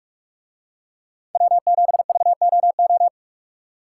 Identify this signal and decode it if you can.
Morse code, heard as a keyed tone, about 44 wpm, 700 Hz. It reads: W7VOO